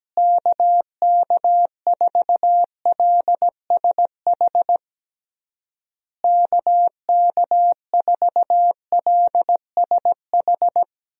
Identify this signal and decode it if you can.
Morse code, heard as a keyed tone, about 17 wpm, 700 Hz. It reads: KK4LSH KK4LSH
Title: USA Call Signs - 6 Characters - Repeated 3x - 17wpm